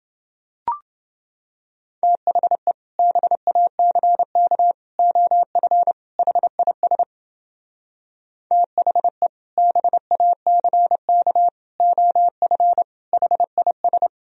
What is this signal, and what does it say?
Morse code, heard as a keyed tone, about 29 wpm, 700 Hz. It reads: THE BACK OF HIS THE BACK OF HIS